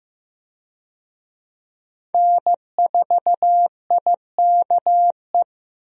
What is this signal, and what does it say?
Morse code, heard as a keyed tone, about 15 wpm, 700 Hz. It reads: N4IKE
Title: USA Call Signs - 5 Characters - Repeated 3x - 15wpm